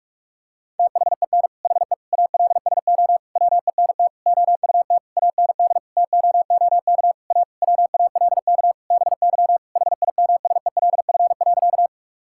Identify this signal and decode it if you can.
Morse code, heard as a keyed tone, about 43 wpm, 700 Hz. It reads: THEN HE ALSO WENT OUT AND TOOK A WALK BY HIMSELF.
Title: Sentences from Top 1600 Words 45wpm